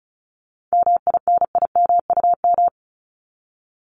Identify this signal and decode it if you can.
Morse code, heard as a keyed tone, about 34 wpm, 700 Hz. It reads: MINIMUM
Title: Top 1251-1500 Words 35wpm